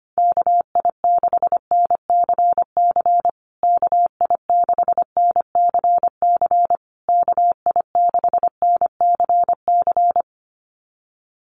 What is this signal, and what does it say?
Morse code, heard as a keyed tone, about 25 wpm, 700 Hz. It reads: KI6NCC KI6NCC KI6NCC
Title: USA Call Signs Repeated 3x 25wpm